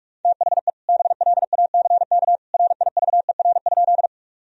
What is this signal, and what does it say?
Morse code, heard as a keyed tone, about 46 wpm, 700 Hz. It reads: THE BLACK RIVER?
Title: Sentences from Top 1000 Words - Review Part 2 - 45wpm